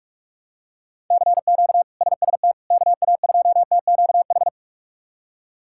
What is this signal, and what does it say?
Morse code, heard as a keyed tone, about 46 wpm, 700 Hz. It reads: CQ SST KA2TQH
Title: K1USN SST QSO Practice - 45wpm